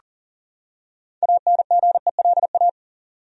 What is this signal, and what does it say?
Morse code, heard as a keyed tone, about 37 wpm, 700 Hz. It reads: ANGELA
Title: USA Common Names - 40wpm